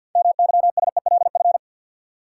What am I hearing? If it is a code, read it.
Morse code, heard as a keyed tone, about 53 wpm, 700 Hz. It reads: MYSELF
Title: Six Letter Words - 50wpm